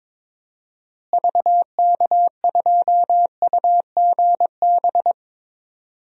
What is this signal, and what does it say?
Morse code, heard as a keyed tone, about 22 wpm, 700 Hz. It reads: VK2UGB
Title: DXCC-100 Call Signs - 6 Characters - Repeated 3x - 22wpm